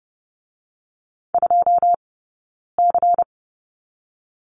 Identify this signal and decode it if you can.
Morse code, heard as a keyed tone, about 30 wpm, 700 Hz. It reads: 2 C
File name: Lesson 16d - Letter c - Two Characters 30wpm